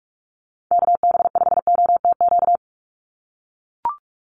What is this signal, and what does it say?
Morse code, heard as a keyed tone, about 44 wpm, 700 Hz. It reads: KB5OTQ
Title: USA Call Signs 45wpm